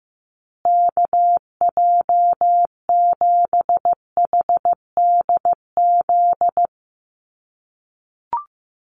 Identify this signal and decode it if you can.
Morse code, heard as a keyed tone, about 15 wpm, 700 Hz. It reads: KJ7HDZ
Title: USA Call Signs Repeated 3x 15wpm